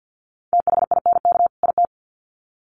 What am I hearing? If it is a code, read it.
Morse code, heard as a keyed tone, about 50 wpm, 700 Hz. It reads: THINK IT
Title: Sets of 2 Words 50wpm